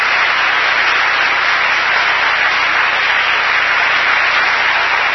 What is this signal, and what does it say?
Emergency Alert System (SAME) tones, heard as harsh buzzing bursts. It is heard from 1.0 to 4.1 s.